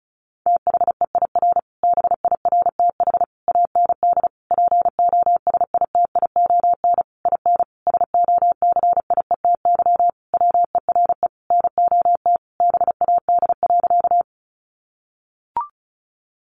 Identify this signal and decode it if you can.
Morse code, heard as a keyed tone, about 34 wpm, 700 Hz. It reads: THEIR BIRTH AND POSITION IN SOCIETY WERE NOT BAD.
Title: Sentences from Top 900 Words 35wpm